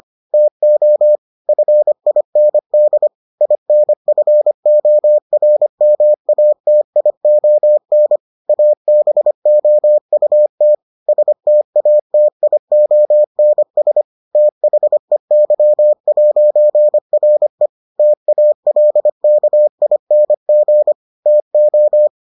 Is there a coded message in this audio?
Yes: Morse code from 0.3 to 22.1 s, heard as a keyed tone, about 25 wpm, 600 Hz. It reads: TO FIND INFORMATION ABOUT STATIONS THEY'RE TALKING TO